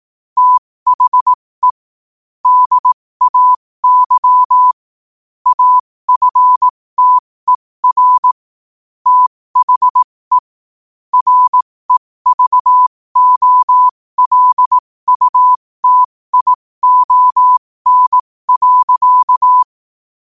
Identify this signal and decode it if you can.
Morse code, heard as a keyed tone, about 18 wpm, spaced as for 15 wpm, 1000 Hz. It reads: THE DAY AFTER THE REVOLUTION.
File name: qotdfiles / 15wpm